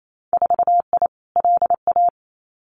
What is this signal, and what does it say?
Morse code, heard as a keyed tone, about 28 wpm, 700 Hz. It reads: 4I LA
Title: Winter Field Day Exchanges Encore 28wpm